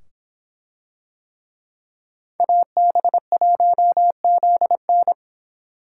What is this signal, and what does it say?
Morse code, heard as a keyed tone, about 25 wpm, 700 Hz. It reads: AB1ZN